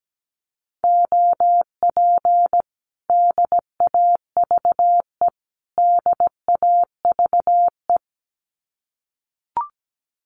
Morse code, heard as a keyed tone, about 17 wpm, 700 Hz. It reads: OP DAVE DAVE
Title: Common QSO Phrases Encore 17wpm